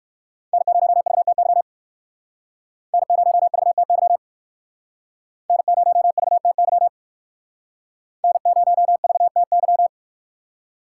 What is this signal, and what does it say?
Morse code, heard as a keyed tone, about 54 wpm, 700 Hz. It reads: N0VTY N0VTY N0VTY N0VTY
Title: Speed Racing - USA Call Signs 45wpm